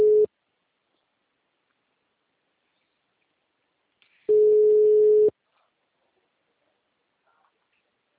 Ringback tone (Europe, 425 Hz).
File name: out- -2008- - - .76